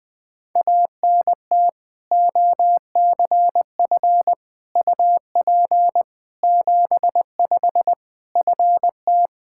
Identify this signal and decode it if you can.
Morse code, heard as a keyed tone, about 20 wpm, 700 Hz. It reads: ANT OCF UP 75 FT
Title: Practice QSO - Exchange 2 - Info - 20wpm